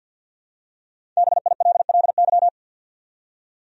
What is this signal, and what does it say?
Morse code, heard as a keyed tone, about 51 wpm, 700 Hz. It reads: BILLY
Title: USA Common Names - 50wpm